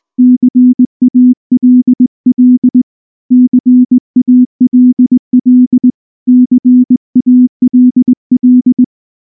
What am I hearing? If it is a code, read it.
Morse code, heard as a keyed tone, about 19 wpm, 250 Hz. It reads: CALL CALL CALL
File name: morse / morse-training-20WPM-3